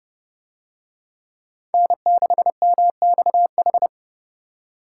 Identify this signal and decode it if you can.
Morse code, heard as a keyed tone, about 29 wpm, 700 Hz. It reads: N6MXH